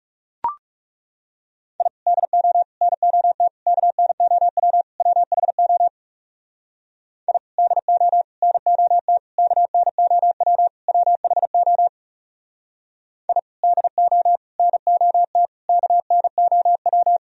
Morse code, heard as a keyed tone, about 36 wpm, 700 Hz. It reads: I DO NOT KNOW WHO I DO NOT KNOW WHO I DO NOT KNOW
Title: Speed-Racing - Sets of 5 Words 30wpm